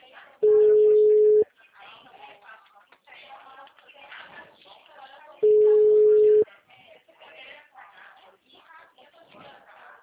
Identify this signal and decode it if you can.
Ringback tone (Europe, 425 Hz), with speech.
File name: out- -2022- - - .33389